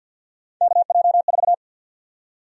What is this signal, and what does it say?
Morse code, heard as a keyed tone, about 50 wpm, 700 Hz. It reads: KJ4
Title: USA Call Sign Prefixes - 50wpm